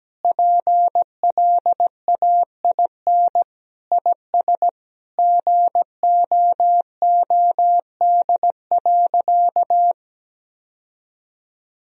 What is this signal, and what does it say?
Morse code, heard as a keyed tone, about 17 wpm, 700 Hz. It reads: PLAIN IS GOOD.
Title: Sentences from Top 1000 Words - Review Part 4 - 17wpm